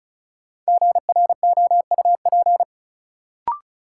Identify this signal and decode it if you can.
Morse code, heard as a keyed tone, about 36 wpm, 700 Hz. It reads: GROUP
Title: Sentences from Top 1000 Words - Review Part 1 - New Word Drill - 35wpm